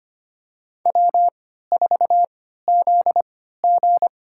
Morse code, heard as a keyed tone, about 25 wpm, spaced as for 14 wpm, 700 Hz. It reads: W4ZG
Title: USA Call Signs - 4 Characters - 3x Character Spacing - 25wpm